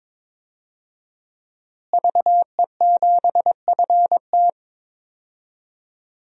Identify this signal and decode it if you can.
Morse code, heard as a keyed tone, about 21 wpm, 700 Hz. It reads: VE7FT
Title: DXCC-100 Call Signs - 5 Characters - Repeated 2x - 22wpm